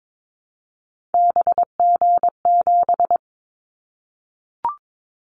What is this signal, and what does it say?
Morse code, heard as a keyed tone, about 22 wpm, 700 Hz. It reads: BG7
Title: Callsign Prefixes for Top 100 DXCC - 22wpm